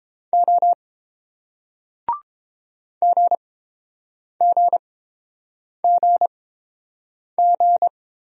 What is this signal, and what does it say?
Morse code, heard as a keyed tone, about 30 wpm, spaced as for 7 wpm, 700 Hz. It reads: O GGGG